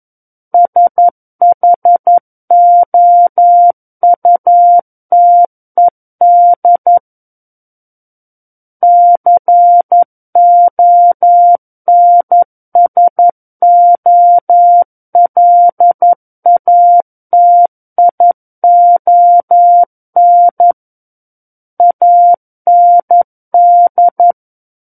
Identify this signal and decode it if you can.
Morse code, heard as a keyed tone, about 11 wpm, 700 Hz. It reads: SHOUTED CONSOLATION AND
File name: 11 WPM